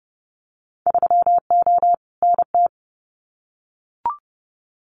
Morse code, heard as a keyed tone, about 30 wpm, 700 Hz. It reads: 3O NT
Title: Winter Field Day Exchanges Encore 30wpm